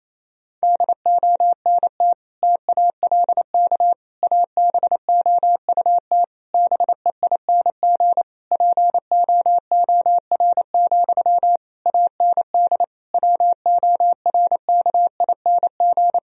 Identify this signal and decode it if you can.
Morse code, heard as a keyed tone, about 28 wpm, 700 Hz. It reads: DONT TALK ABOUT BEING POOR, AND WORKING
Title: Sentences from Top 1000 Words - Review Part 4 - 28wpm